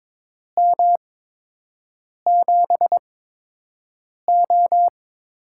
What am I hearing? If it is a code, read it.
Morse code, heard as a keyed tone, about 22 wpm, spaced as for 6 wpm, 700 Hz. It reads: M7O
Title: USA Call Signs - 4 Characters - 8x Character Spacing - 22wpm